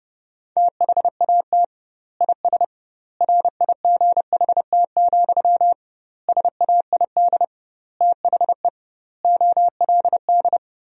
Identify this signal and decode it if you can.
Morse code, heard as a keyed tone, about 32 wpm, 700 Hz. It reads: THAT IS RIGHT, SAID THE OLD
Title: 2x Word-Spacing - Sentences from Top 200 Words - 30wpm